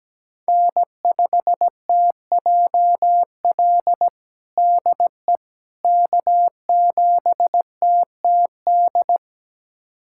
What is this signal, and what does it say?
Morse code, heard as a keyed tone, about 17 wpm, 700 Hz. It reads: N5TJL DE K7TTD
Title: USA Callsign DE Callsign - 5 Characters - 17wpm